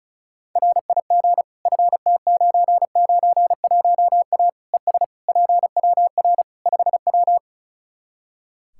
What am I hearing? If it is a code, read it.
Morse code, heard as a keyed tone, about 34 wpm, 700 Hz. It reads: RIG FT991A ES PWR 5W
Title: Practice QSO - Exchange 2 - Info - 35wpm